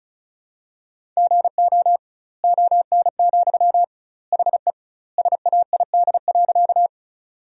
Morse code, heard as a keyed tone, about 36 wpm, 700 Hz. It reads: GO ON, HE SAID.
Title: 2x Word-Spacing - Sentences from Top 100 Words - 35wpm